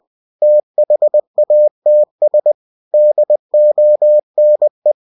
Morse code, heard as a keyed tone, about 20 wpm, 600 Hz. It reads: THATS DONE
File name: cw-books / winnie-the-pooh / winnie-20wpm-page-0054